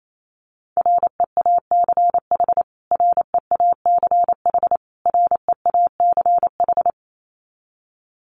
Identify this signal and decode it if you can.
Morse code, heard as a keyed tone, about 28 wpm, 700 Hz. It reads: REACH REACH REACH
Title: Top 500 Words Repeated 28wpm